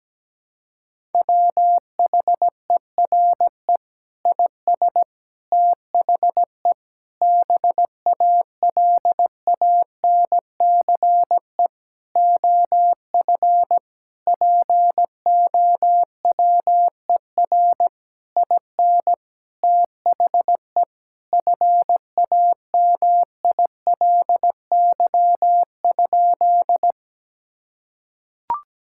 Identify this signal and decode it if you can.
Morse code, heard as a keyed tone, about 17 wpm, 700 Hz. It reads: WHERE IS THE BALANCE OF POWER IN THE FAMILY?
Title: Sentences from Top 1000 Words - Review Part 6 - 17wpm